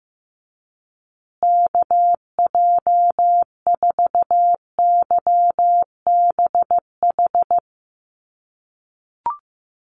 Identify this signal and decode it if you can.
Morse code, heard as a keyed tone, about 15 wpm, 700 Hz. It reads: KJ4YBH
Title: USA Call Signs Repeated 2x 15wpm